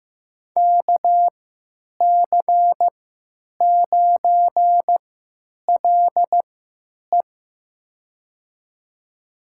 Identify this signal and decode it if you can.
Morse code, heard as a keyed tone, about 15 wpm, spaced as for 9 wpm, 700 Hz. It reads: KC9LE
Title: USA Call Signs - 5 Characters - 3x Character Spacing - 15wpm